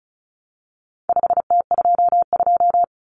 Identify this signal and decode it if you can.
Morse code, heard as a keyed tone, about 34 wpm, 700 Hz. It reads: 5T22
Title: Common Cut Numbers - 4-Digit 35wpm